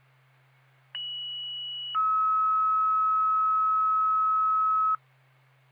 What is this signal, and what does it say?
Two-tone page, heard as a one-second tone, then a three-second tone, about 2950 then 1290 Hz. It is heard from 0.9 to 5.0 s.